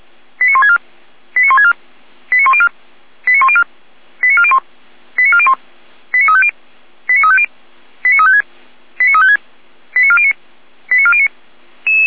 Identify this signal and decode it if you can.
Five-tone selective call (ZVEI), heard as quick runs of five tones, tones of about 80 ms. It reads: 89164 89164 89104 89104 89401 89401 89350 89350 89356 89356 89409 89409